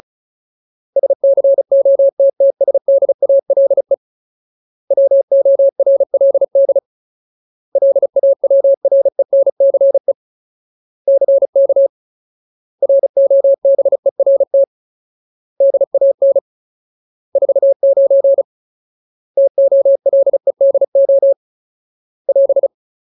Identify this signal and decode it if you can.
Morse code, heard as a keyed tone, about 34 wpm, 550 Hz. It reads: SCOTTSDALE WORLD LAWRENCE CK ROBERT DAN 49 TOLEDO <AS>
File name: Ultimate Word List / 35WPM Ultimate